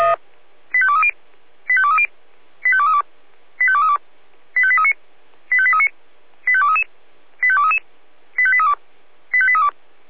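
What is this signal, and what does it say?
Five-tone selective call (ZVEI), heard as quick runs of five tones, tones of about 80 ms. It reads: 85230 85230 85232 85232 85839 85839 85233 85233 85832 85832
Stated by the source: ZVEI five-tone selective call received with an SDR.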